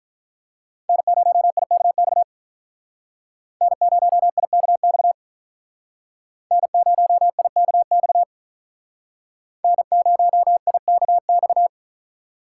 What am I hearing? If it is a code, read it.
Morse code, heard as a keyed tone, about 45 wpm, 700 Hz. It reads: N0IKX N0IKX N0IKX N0IKX